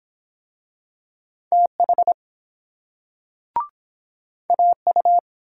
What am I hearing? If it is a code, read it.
Morse code, heard as a keyed tone, about 25 wpm, 700 Hz. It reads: TH AU